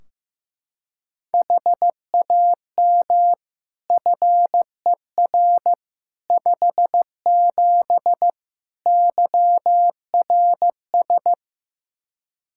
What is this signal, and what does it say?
Morse code, heard as a keyed tone, about 15 wpm, 700 Hz. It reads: HAM FER 57 YRS
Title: K1USN SST QSO Practice 15wpm